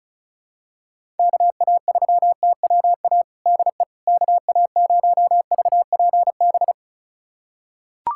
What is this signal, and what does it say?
Morse code, heard as a keyed tone, about 36 wpm, 700 Hz. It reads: KA3TWA DE KA0VPB
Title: USA Callsign DE Callsign - 6 Characters - 35wpm